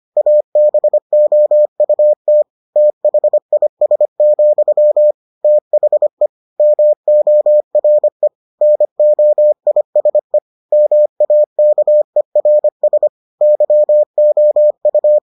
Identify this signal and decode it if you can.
Morse code, heard as a keyed tone, about 25 wpm, 600 Hz. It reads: ABOUT THIS, THE MORE NOISE MAKERS YOU